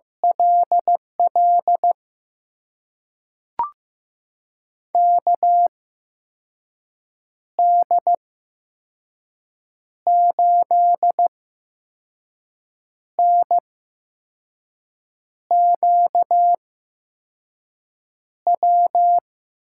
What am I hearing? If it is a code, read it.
Morse code, heard as a keyed tone, about 15 wpm, spaced as for 5 wpm, 700 Hz. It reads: LL KD8NQW
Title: USA Call Signs - 6 Characters - 8x Character Spacing - 15wpm